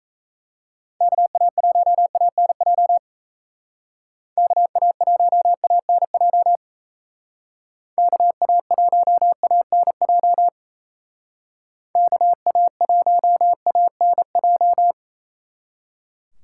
Morse code, heard as a keyed tone, about 34 wpm, 700 Hz. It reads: KA1ANJ KA1ANJ KA1ANJ KA1ANJ
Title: Speed Racing - USA 2x3 Call Signs 28wpm